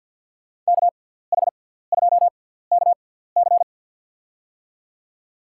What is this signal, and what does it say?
Morse code, heard as a keyed tone, about 53 wpm, spaced as for 17 wpm, 700 Hz. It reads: KH2KC
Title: USA Call Signs - 6x Character Spacing - 50wpm